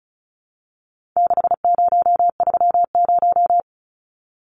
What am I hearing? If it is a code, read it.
Morse code, heard as a keyed tone, about 34 wpm, 700 Hz. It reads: 6030
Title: Numbers - 4-Digit 35wpm